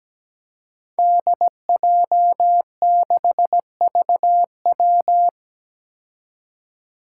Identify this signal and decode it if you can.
Morse code, heard as a keyed tone, about 17 wpm, 700 Hz. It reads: DJ6VW
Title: Five Challenging Letter-Number Combinations 17wpm